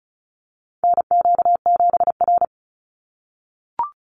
Morse code, heard as a keyed tone, about 34 wpm, 700 Hz. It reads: NQ7R
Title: NA SOTA Chaser Callsigns 35wpm